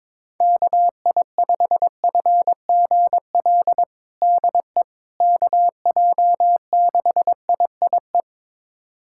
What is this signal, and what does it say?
Morse code, heard as a keyed tone, about 22 wpm, 700 Hz. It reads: KI5FGL DE KJ6IIE
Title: USA Callsign DE Callsign - 6 Characters - 22wpm